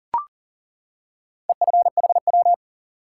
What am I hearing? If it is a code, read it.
Morse code, heard as a keyed tone, about 40 wpm, 700 Hz. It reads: EFHW
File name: IQE Course 3 - Lesson 89 - YAGI - Two Elements - 40wpm